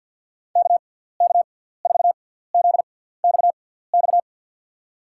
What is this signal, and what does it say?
Morse code, heard as a keyed tone, about 46 wpm, spaced as for 17 wpm, 700 Hz. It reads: KK4ZXX
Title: USA Call Signs - 6 Characters - 6x Character Spacing - 50wpm